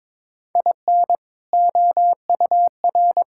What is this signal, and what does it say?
Morse code, heard as a keyed tone, about 22 wpm, 700 Hz. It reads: IN OUR
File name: practice / 22wpm / Sentences with 2-Word Phrases - 22wpm